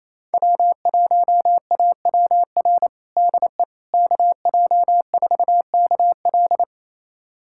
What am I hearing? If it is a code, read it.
Morse code, heard as a keyed tone, about 28 wpm, 700 Hz. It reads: W1AWR DE KJ4KL
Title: USA Callsign DE Callsign - 5 Characters - 28wpm